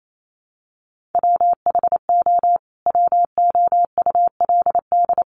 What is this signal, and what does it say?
Morse code, heard as a keyed tone, about 28 wpm, 700 Hz. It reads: WHO WOULD
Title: Sentences from Top 1400 Words 28wpm